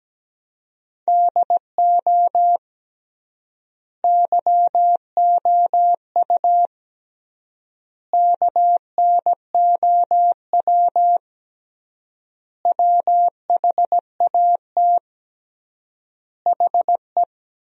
Morse code, heard as a keyed tone, about 17 wpm, 700 Hz. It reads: DO YOU KNOW WHAT HE
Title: 3x Word-Spacing - Sentences from Top 300 Words - 17wpm